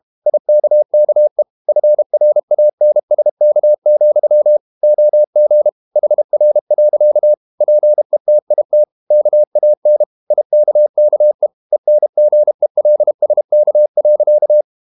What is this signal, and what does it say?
Morse code, heard as a keyed tone, about 32 wpm, 600 Hz. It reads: IKKE FRANSK, OG HR. PETIT KAN IKKE ENGELSK.
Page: Ferie 32 wpm